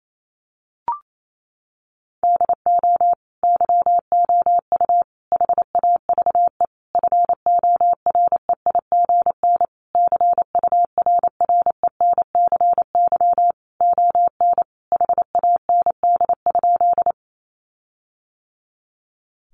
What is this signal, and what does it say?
Morse code, heard as a keyed tone, about 28 wpm, 700 Hz. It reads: DO YOU HAVE FOREIGN CURRENCY ON HAND?
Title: Sentences from Top 1500 Words 28wpm